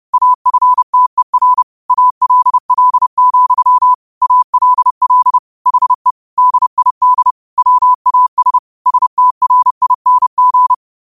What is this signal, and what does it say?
Morse code, heard as a keyed tone, about 30 wpm, 1000 Hz. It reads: AFTER ALL, ALL HE DID WAS STRING